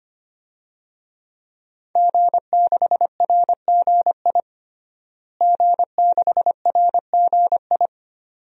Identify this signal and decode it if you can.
Morse code, heard as a keyed tone, about 25 wpm, 700 Hz. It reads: G6RGI G6RGI
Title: DXCC-100 Call Signs - 5 Characters - Repeated 2x - 25wpm